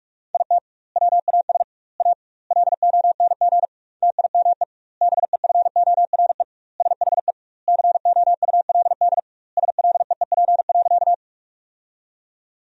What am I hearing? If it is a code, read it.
Morse code, heard as a keyed tone, about 46 wpm, 700 Hz. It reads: IT WAS A LONG TIME BEFORE SHE COULD SLEEP.
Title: 2x Word-Spacing - Sentences from Top 500 Words - 45wpm